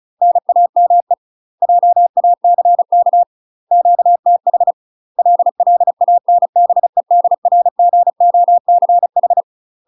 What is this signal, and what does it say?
Morse code, heard as a keyed tone, about 36 wpm, 700 Hz. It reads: NAME JACK QTH LLANBEDRGOCH
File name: UK MASTER QSOs TRAINING / 35WPM / UK MASTER QSOs 0019